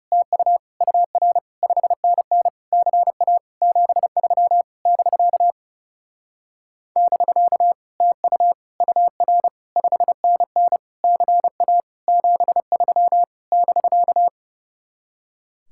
Morse code, heard as a keyed tone, about 34 wpm, 700 Hz. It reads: TU UR 5NN CA 73 <BK> <BK> TU UR 5NN CA 73 <BK>
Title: Speed-Racing Common QSO Phrases Encore - 30wpm